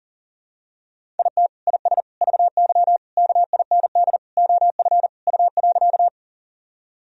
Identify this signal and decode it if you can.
Morse code, heard as a keyed tone, about 40 wpm, 700 Hz. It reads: IT IS VY KIND OF U.
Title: Sentences from Top 200 Words and Abbreviations 40wpm